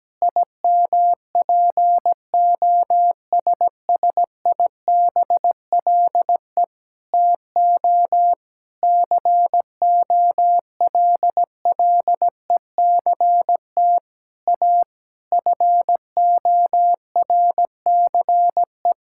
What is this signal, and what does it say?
Morse code, heard as a keyed tone, about 17 wpm, 700 Hz. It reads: IMPOSSIBLE TO COLLECT A FORCE
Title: Sentences from Top 1600 Words 17wpm